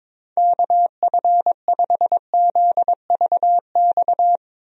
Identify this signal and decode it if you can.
Morse code, heard as a keyed tone, about 22 wpm, 700 Hz. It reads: KF5ZVX
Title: USA Call Signs - 6 Characters - 22wpm